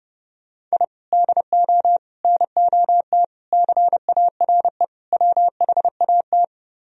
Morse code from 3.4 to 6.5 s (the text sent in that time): CARE WHAT